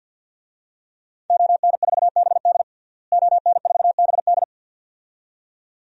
Morse code, heard as a keyed tone, about 48 wpm, 700 Hz. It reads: ON4BD ON4BD
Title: DXCC-100 Call Signs - 5 Characters - Repeated 2x - 50wpm